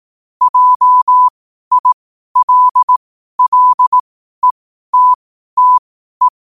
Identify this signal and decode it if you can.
Morse code, heard as a keyed tone, about 18 wpm, spaced as for 12 wpm, 1000 Hz. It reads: JILLETTE
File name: qotdfiles / 12wpm